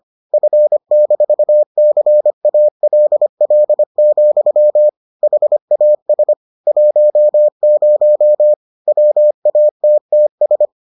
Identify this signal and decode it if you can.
Morse code, heard as a keyed tone, about 25 wpm, 600 Hz. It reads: F-CALL, HAS 10 WATTS